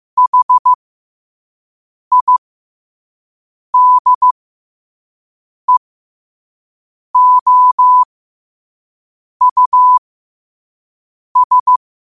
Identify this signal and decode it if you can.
Morse code, heard as a keyed tone, about 15 wpm, spaced as for 5 wpm, 1000 Hz. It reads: HIDEOUS
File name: qotdfiles / 5wpm